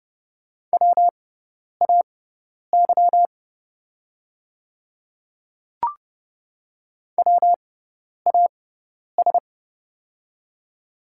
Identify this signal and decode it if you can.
Morse code, heard as a keyed tone, about 30 wpm, spaced as for 10 wpm, 700 Hz. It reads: WAY WAS
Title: 6x Character Spacing - Top 100 Words - 30wpm